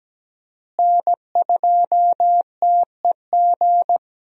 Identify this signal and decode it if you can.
Morse code, heard as a keyed tone, about 17 wpm, 700 Hz. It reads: N2TEG
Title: USA Call Signs - 5 Characters - Repeated 2x - 17wpm